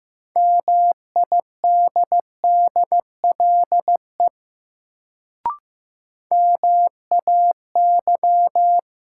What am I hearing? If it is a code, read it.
Morse code, heard as a keyed tone, about 15 wpm, 700 Hz. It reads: MIDDLE MAY
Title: Sets of 2 Words - New Word Drill - 15wpm